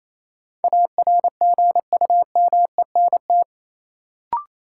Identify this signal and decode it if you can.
Morse code, heard as a keyed tone, about 28 wpm, 700 Hz. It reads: ARGUMENT
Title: Sentences from Top 1400 Words - New Word Drill Alt - 28wpm